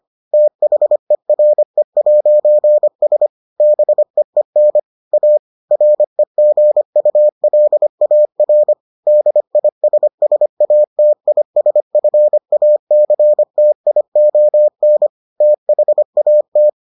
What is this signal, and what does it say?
Morse code, heard as a keyed tone, about 25 wpm, 600 Hz. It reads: THERE'S BEEN A REGULAR DISSATISFACTION THAT